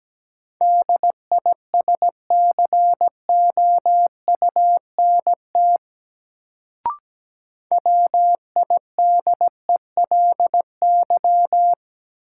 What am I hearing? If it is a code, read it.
Morse code, heard as a keyed tone, about 17 wpm, 700 Hz. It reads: DISCOUNT WIDELY
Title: Sentences from Top 1200 Words - New Word Drill Alt - 17wpm